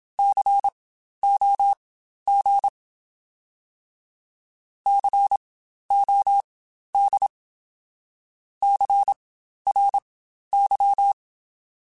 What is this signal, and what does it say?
Morse code, heard as a keyed tone, about 27 wpm, spaced as for 12 wpm, 800 Hz. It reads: COG COD CRY